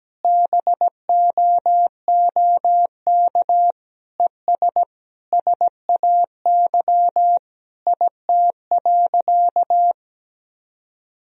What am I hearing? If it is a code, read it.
Morse code, heard as a keyed tone, about 17 wpm, 700 Hz. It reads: BOOK ES SAY IT.
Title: Sentences from Top 400 Words and Abbreviations 17wpm